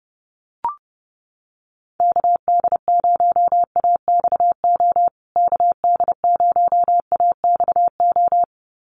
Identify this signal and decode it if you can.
Morse code, heard as a keyed tone, about 30 wpm, 700 Hz. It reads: KD0AXO KD0AXO
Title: USA Call Signs Repeated 2x 30wpm